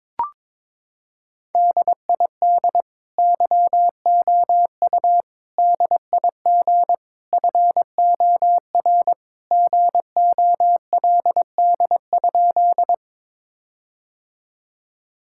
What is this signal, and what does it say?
Morse code, heard as a keyed tone, about 22 wpm, 700 Hz. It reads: DID YOU DIG FOR GOLD?
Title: Sentences from Top 1600 Words 22wpm